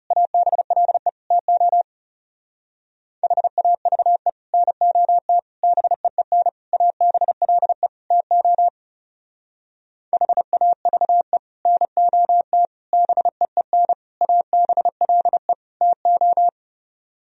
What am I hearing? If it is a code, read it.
Morse code, heard as a keyed tone, about 38 wpm, 700 Hz. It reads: ABLE TO HAVE NOT BEEN ABLE TO HAVE NOT BEEN ABLE TO